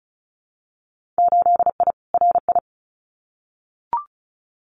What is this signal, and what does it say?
Morse code, heard as a keyed tone, about 36 wpm, 700 Hz. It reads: 8I RI
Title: Winter Field Day Exchanges Encore 35wpm